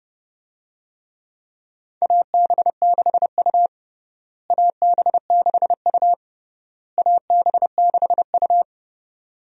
Morse code, heard as a keyed tone, about 30 wpm, 700 Hz. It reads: AB6U AB6U AB6U
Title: USA Call Signs - 4 Characters - Repeated 3x - 30wpm